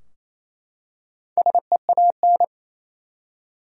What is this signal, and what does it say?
Morse code, heard as a keyed tone, about 28 wpm, 700 Hz. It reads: SEAN
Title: USA Common Names - Max 4 Letters - 28wpm